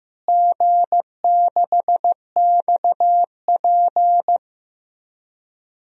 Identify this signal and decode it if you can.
Morse code, heard as a keyed tone, about 15 wpm, 700 Hz. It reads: G6XP
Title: DXCC-100 Call Signs - 4 Characters - Repeated 3x - 15wpm